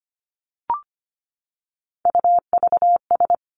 Morse code, heard as a keyed tone, about 25 wpm, 700 Hz. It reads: UVS
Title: Three Challenging Letter-Number Combinations 25wpm